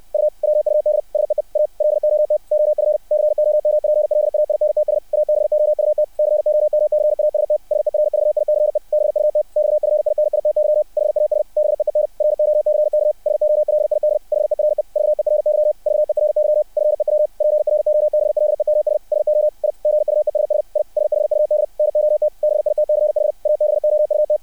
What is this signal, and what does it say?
Morse code, heard as a keyed tone, about 60 wpm, 600 Hz. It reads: U WUD SEE A LOT OF COMPLAINTS ABOUT COMCAST SERVICE BUT XFINITY HAS BEEN ROCK SOLID HERE VERY VERY FEW PROBLEMS SO I GUESS I SHUD NOT BITCH ABOUT
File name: archived / CRSnet